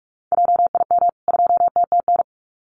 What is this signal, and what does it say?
Morse code, heard as a keyed tone, about 44 wpm, 700 Hz. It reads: JIM 2TTN
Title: CWT Exchange 45wpm